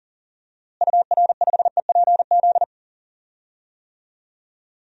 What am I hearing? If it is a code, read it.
Morse code, heard as a keyed tone, about 40 wpm, 700 Hz. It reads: UR5EPZ
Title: DXCC-100 Call Signs - 6 Characters - Repeated 2x - 40wpm